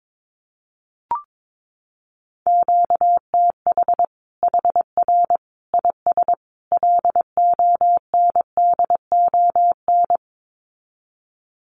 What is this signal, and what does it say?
Morse code, heard as a keyed tone, about 22 wpm, 700 Hz. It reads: QTH HR IS LONDON
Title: Common QSO Phrases 22wpm